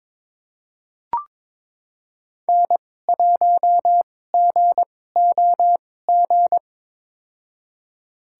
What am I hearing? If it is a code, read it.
Morse code, heard as a keyed tone, about 22 wpm, spaced as for 16 wpm, 700 Hz. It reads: N1GOG
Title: USA Call Signs - 2x Character Spacing - 22wpm